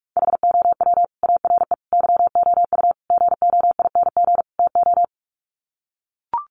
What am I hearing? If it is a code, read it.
Morse code, heard as a keyed tone, about 44 wpm, 700 Hz. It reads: HOW ARE YOU GOING TO
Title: Sets of 5 Words 45wpm